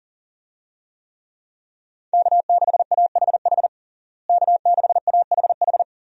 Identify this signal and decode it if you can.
Morse code, heard as a keyed tone, about 40 wpm, 700 Hz. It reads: K6AHH K6AHH
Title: USA Call Signs - 5 Characters - Repeated 3x - 40wpm